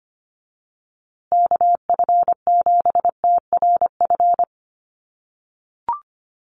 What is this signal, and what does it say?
Morse code, heard as a keyed tone, about 25 wpm, 700 Hz. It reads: KF7TRF
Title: USA Call Signs Repeated 2x 25wpm